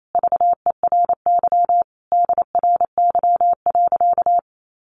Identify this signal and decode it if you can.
Morse code, heard as a keyed tone, about 28 wpm, 700 Hz. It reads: VERY DRY.
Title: Sentences from Top 1400 Words 28wpm